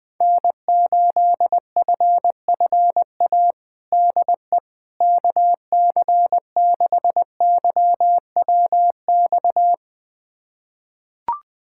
Morse code, heard as a keyed tone, about 20 wpm, 700 Hz. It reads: N8FFA DE KC6YWX
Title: USA Callsign DE Callsign 20wpm